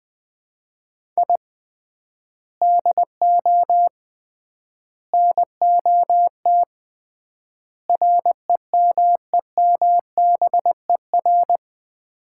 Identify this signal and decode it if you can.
Morse code, heard as a keyed tone, about 20 wpm, 700 Hz. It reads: I DO NOT REMEMBER
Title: 3x Word-Spacing - Sets of 4 Words - 20wpm